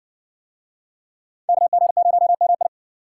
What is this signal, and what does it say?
Morse code, heard as a keyed tone, about 66 wpm, 700 Hz. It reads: KG0MI